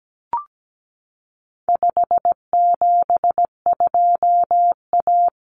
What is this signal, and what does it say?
Morse code, heard as a keyed tone, about 17 wpm, 700 Hz. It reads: 572A
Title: Common Cut Numbers - 4-Digit 17wpm